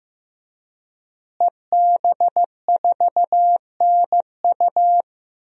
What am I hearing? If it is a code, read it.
Morse code, heard as a keyed tone, about 15 wpm, 700 Hz. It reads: EB4NU
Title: DXCC-100 Call Signs - 5 Characters - Repeated 2x - 15wpm